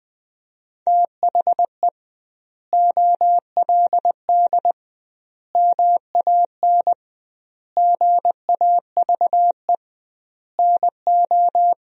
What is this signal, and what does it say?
Morse code, heard as a keyed tone, about 21 wpm, 700 Hz. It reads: THE OLD MAN GAVE NO
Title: 2x Word-Spacing - Sentences from Top 500 Words - 20wpm